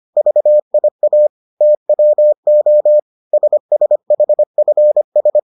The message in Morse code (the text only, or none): VIA TWO SSHFS